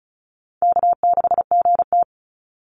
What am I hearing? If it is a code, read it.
Morse code, heard as a keyed tone, about 34 wpm, 700 Hz. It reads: K6GT
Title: Easy Call Signs Repeated 2x - 35wpm